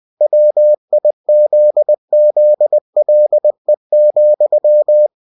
Morse code, heard as a keyed tone, about 20 wpm, 600 Hz. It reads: WIZZLE,
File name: cw-books / winnie-the-pooh / winnie-20wpm-page-0023